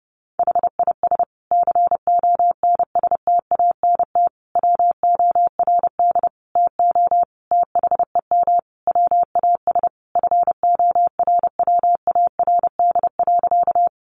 Morse code, heard as a keyed tone, about 30 wpm, 700 Hz. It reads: HIS CONSTANT WORD TO THEM WAS FORWARD.
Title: Sentences from Top 1500 Words 30wpm